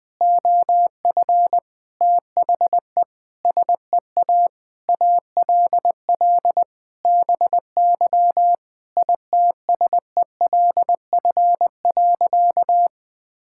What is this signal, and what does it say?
Morse code, heard as a keyed tone, about 21 wpm, 700 Hz. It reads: OF THE SEA ALL BY ITSELF.
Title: Sentences from Top 1000 Words - Review Part 5 - 20wpm